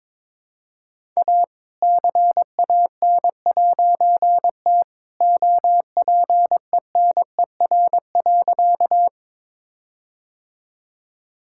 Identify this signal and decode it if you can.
Morse code, heard as a keyed tone, about 21 wpm, 700 Hz. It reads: A CAN'T OPENER.
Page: morse-code-ninja-practice-22wpm